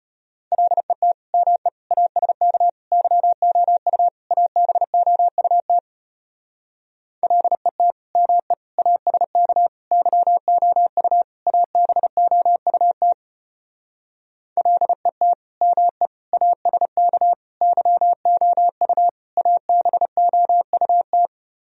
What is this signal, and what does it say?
Morse code, heard as a keyed tone, about 33 wpm, 700 Hz. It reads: LET ME ASK YOU ABOUT LET ME ASK YOU ABOUT LET ME ASK YOU ABOUT